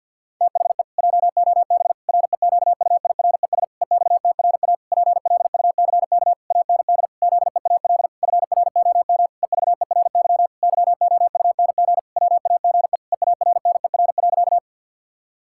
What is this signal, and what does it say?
Morse code, heard as a keyed tone, about 49 wpm, 700 Hz. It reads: THE JOB REQUIRES EXTRA PLUCK AND ZEAL FROM EVERY YOUNG WAGE EARNER.
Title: Pangram Challenge v2 - 50wpm